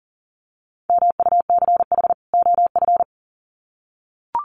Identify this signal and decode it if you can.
Morse code, heard as a keyed tone, about 40 wpm, 700 Hz. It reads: MUCH OF
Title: Sets of 2 Words 40wpm